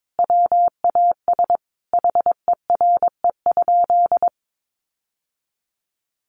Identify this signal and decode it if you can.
Morse code, heard as a keyed tone, about 22 wpm, 700 Hz. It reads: WAS HERE?
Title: Sentences from Top 1500 Words 22wpm